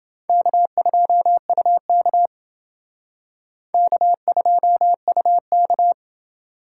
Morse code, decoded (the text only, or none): K2UK K2UK